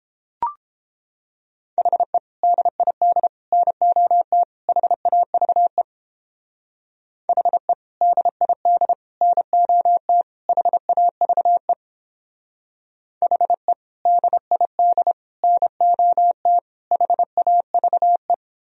Morse code, heard as a keyed tone, about 31 wpm, 700 Hz. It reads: HE DID NOT HAVE HE DID NOT HAVE HE DID NOT HAVE